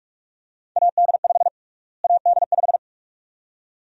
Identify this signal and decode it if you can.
Morse code, heard as a keyed tone, about 47 wpm, 700 Hz. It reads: AD5 AD5
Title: USA Call Sign Prefixes - Repeated 2x - 45wpm